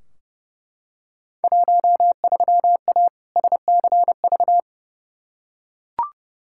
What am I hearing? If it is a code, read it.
Morse code, heard as a keyed tone, about 32 wpm, 700 Hz. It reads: 13A SCV
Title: Field Day Exchanges Encore - 30wpm